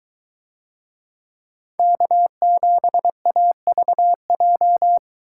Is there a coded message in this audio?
Yes: Morse code from 1.8 to 5.0 s, heard as a keyed tone, about 22 wpm, 700 Hz. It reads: K7AVJ